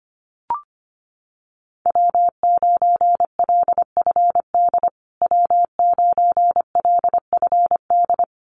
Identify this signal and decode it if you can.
Morse code, heard as a keyed tone, about 25 wpm, 700 Hz. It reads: W9LFD W9LFD
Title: USA Call Signs Repeated 2x 25wpm